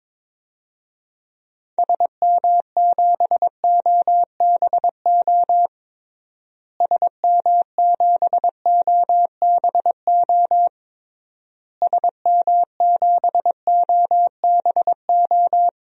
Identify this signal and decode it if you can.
Morse code, heard as a keyed tone, about 22 wpm, 700 Hz. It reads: SM7OBO SM7OBO SM7OBO
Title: DXCC-100 Call Signs - 6 Characters - Repeated 3x - 22wpm